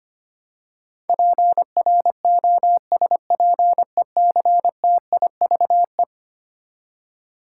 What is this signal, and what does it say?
Morse code, heard as a keyed tone, about 25 wpm, 700 Hz. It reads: PROSPECTIVE
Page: morse-code-ninja-practice-25wpm